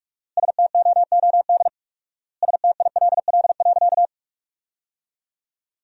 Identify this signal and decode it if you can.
Morse code, heard as a keyed tone, about 45 wpm, 690 Hz. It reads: STOOD STILL.
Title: 4x Word-Spacing - Sentences from Top 500 Words - 45wpm